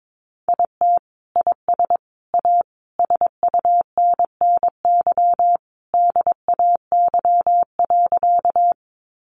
Morse code, heard as a keyed tone, about 22 wpm, 700 Hz. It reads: IT IS A SUNNY DAY.
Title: Easy Phrases 22wpm